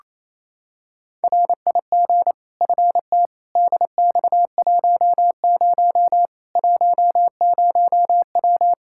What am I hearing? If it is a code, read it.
Morse code, heard as a keyed tone, about 28 wpm, 700 Hz. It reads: RIG FT DX10 10W
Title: Practice QSO - Exchange 2 - Info - 28wpm